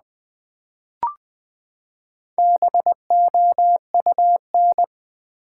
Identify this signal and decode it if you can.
Morse code, heard as a keyed tone, about 20 wpm, 700 Hz. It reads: BOUN
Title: Build A Word - 4 Letters At A Time - 8-Letter Words - 20wpm